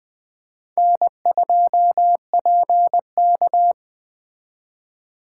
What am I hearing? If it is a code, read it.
Morse code, heard as a keyed tone, about 20 wpm, 700 Hz. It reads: N2PK
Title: Speed Racing - USA 1x2 Call Signs 17wpm